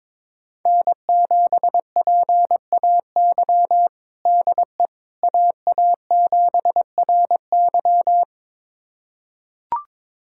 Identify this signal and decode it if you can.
Morse code, heard as a keyed tone, about 22 wpm, 700 Hz. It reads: N7PAY DE AA7RY
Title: USA Callsign DE Callsign - 5 Characters - 22wpm